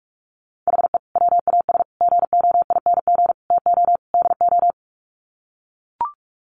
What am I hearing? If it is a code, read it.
Morse code, heard as a keyed tone, about 44 wpm, 700 Hz. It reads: HE WAS GOING TO DO
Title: Sets of 5 Words Encore2 45wpm